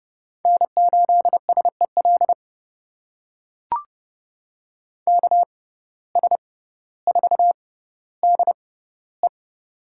Morse code, heard as a keyed tone, about 30 wpm, spaced as for 15 wpm, 700 Hz. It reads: N8SEL KS4DE
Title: USA Call Signs - 5 Characters - 6x Character Spacing - 30wpm